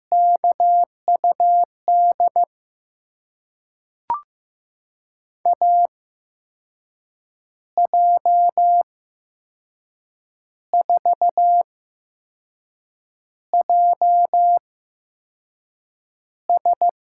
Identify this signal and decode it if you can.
Morse code, heard as a keyed tone, about 15 wpm, spaced as for 5 wpm, 700 Hz. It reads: KUD AJ4JS
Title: USA Call Signs - 5 Characters - 8x Character Spacing - 15wpm